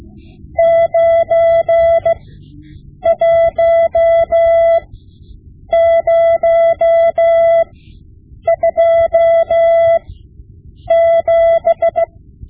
Morse code, heard as a keyed tone, about 16 wpm, spaced as for 7 wpm, 670 Hz. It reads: 91027
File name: Morse Lessons / Morse Lessons / Tape-5B